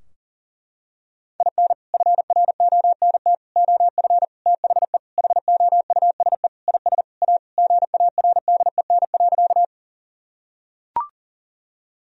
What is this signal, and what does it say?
Morse code, heard as a keyed tone, about 37 wpm, 700 Hz. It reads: IN FRONT OF THE HOUSE IS A GARDEN.
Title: Sentences from Top 1000 Words - Review Part 4 - 40wpm